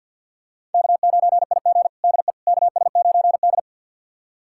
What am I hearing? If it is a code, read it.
Morse code, heard as a keyed tone, about 46 wpm, 700 Hz. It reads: K9IG DE KS9D
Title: USA Callsign DE Callsign - 4 Characters - 50wpm